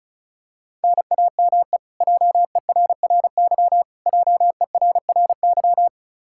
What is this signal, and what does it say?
Morse code, heard as a keyed tone, about 36 wpm, 700 Hz. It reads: NAME JERRY JERRY
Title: Practice QSO - Exchange 1 - Beginning - 35wpm